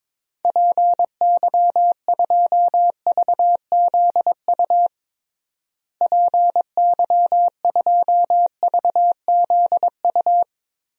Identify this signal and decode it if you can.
Morse code, heard as a keyed tone, about 22 wpm, 700 Hz. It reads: PY2VZU PY2VZU
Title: DXCC-100 Call Signs - 6 Characters - Repeated 3x - 22wpm